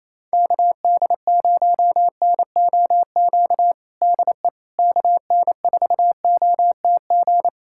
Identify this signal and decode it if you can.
Morse code, heard as a keyed tone, about 28 wpm, 700 Hz. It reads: KD0NOQ DE KN4OTG
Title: USA Callsign DE Callsign - 6 Characters - 28wpm